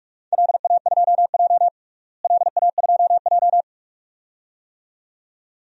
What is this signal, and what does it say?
Morse code, heard as a keyed tone, about 46 wpm, 700 Hz. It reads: LA2J LA2J
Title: DXCC-100 Call Signs - 4 Characters - Repeated 2x - 45wpm